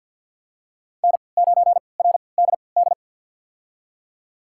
Morse code, heard as a keyed tone, about 47 wpm, spaced as for 28 wpm, 700 Hz. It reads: N9RDD
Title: USA Call Signs - 5 Characters - 3x Character Spacing - 50wpm